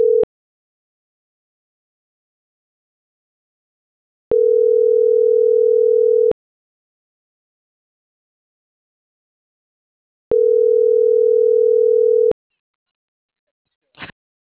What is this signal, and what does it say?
Ringback tone (North America), 440 + 480 Hz.
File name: CENTRAL MAERSK / out alan.vicente